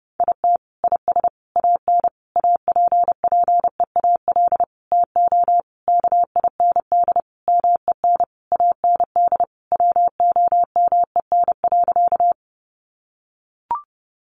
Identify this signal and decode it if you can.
Morse code, heard as a keyed tone, about 30 wpm, 700 Hz. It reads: IT IS AN APPEAL TO KIND MEN AND WOMEN.
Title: Sentences from Top 1300 Words 30wpm